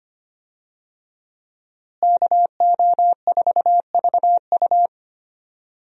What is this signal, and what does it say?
Morse code, heard as a keyed tone, about 25 wpm, 700 Hz. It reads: KO4VU
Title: USA Call Signs - 5 Characters - Repeated 2x - 25wpm